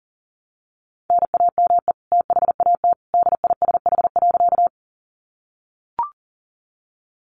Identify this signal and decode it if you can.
Morse code, heard as a keyed tone, about 40 wpm, 700 Hz. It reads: NAME THAT DISH.
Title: Sentences from Top 1300 Words 40wpm